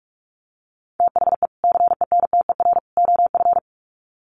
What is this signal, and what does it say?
Morse code, heard as a keyed tone, about 44 wpm, 700 Hz. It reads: THE CENTER OF
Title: Sets of 3 Words 45wpm